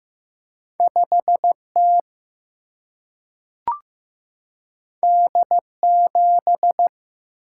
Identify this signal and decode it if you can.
Morse code, heard as a keyed tone, about 15 wpm, 700 Hz. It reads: 5T D7
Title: Single-Letter Single-Number 15wpm